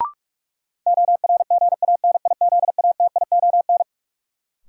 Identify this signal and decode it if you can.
Morse code, heard as a keyed tone, about 46 wpm, 700 Hz. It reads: ORGANIZATION
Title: Sentences from Top 1100 Words - New Word Drill Alt - 45wpm